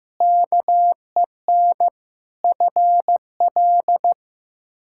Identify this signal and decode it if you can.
Morse code, heard as a keyed tone, about 15 wpm, 700 Hz. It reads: KEN FL
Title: K1USN SST QSO Practice - 15wpm